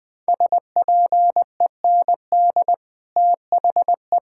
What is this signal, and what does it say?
Morse code, heard as a keyed tone, about 20 wpm, 700 Hz. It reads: SPEND THE
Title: Sets of 8 Words - 20wpm